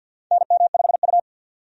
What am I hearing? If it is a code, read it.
Morse code, heard as a keyed tone, about 52 wpm, 700 Hz. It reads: NM5U
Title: USA Call Signs - 4 Characters - 50wpm